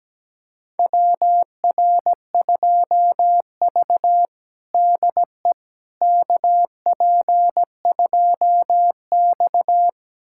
Morse code, heard as a keyed tone, about 17 wpm, 700 Hz. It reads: WR2V DE KP2X
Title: USA Callsign DE Callsign - 4 Characters - 17wpm